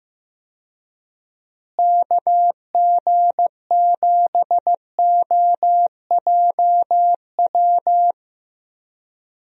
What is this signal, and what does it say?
Morse code, heard as a keyed tone, about 15 wpm, 700 Hz. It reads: KG7OJW
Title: USA Call Signs - 6 Characters - Repeated 2x - 15wpm